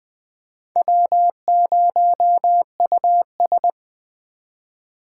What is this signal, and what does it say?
Morse code, heard as a keyed tone, about 20 wpm, 700 Hz. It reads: W0US